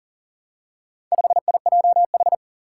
Morse code, heard as a keyed tone, about 38 wpm, 700 Hz. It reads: 5IJH
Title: Four Challenging Letter-Number Combinations 40wpm